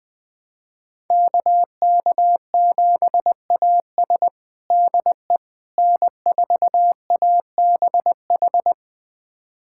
Morse code, heard as a keyed tone, about 20 wpm, 700 Hz. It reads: KK7AS DE N4ABH
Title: USA Callsign DE Callsign - 5 Characters - 20wpm